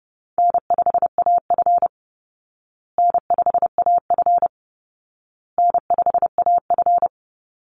Morse code, heard as a keyed tone, about 30 wpm, 700 Hz. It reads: N5AF N5AF N5AF
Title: Easy Call Signs Repeated 3x - 30wpm